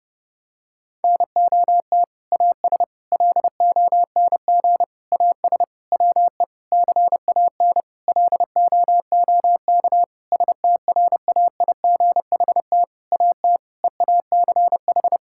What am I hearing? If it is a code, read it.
Morse code, heard as a keyed tone, about 30 wpm, 700 Hz. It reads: NOT AS LONG AS WE CAN LOOK STRAIGHT AT EACH
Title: Sentences from Top 1000 Words - Review Part 6 - 30wpm